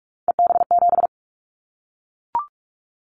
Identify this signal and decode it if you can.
Morse code, heard as a keyed tone, about 44 wpm, 700 Hz. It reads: EB7
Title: Callsign Prefixes for Top 25 DXCC - 45wpm